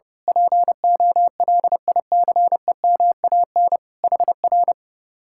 Morse code, heard as a keyed tone, about 28 wpm, 700 Hz. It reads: POLICEMAN HR
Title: Speed-Racing Common QSO Phrases Encore - 30wpm